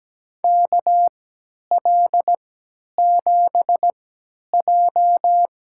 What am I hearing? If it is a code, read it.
Morse code, heard as a keyed tone, about 17 wpm, spaced as for 10 wpm, 700 Hz. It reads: KL7J
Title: USA Call Signs - 3x Character Spacing - 17wpm